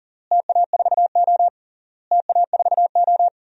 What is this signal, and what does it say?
Morse code, heard as a keyed tone, about 40 wpm, 700 Hz. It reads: TA4O TA4O
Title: DXCC-100 Call Signs - 4 Characters - Repeated 3x - 40wpm